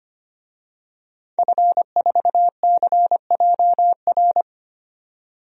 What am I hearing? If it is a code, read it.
Morse code, heard as a keyed tone, about 25 wpm, 700 Hz. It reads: F4CJR
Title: DXCC-100 Call Signs - 5 Characters - 25wpm